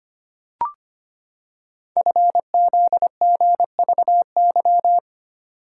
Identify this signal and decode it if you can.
Morse code, heard as a keyed tone, about 25 wpm, 700 Hz. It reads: FZGVY
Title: Code Groups - 5 Letters - 25wpm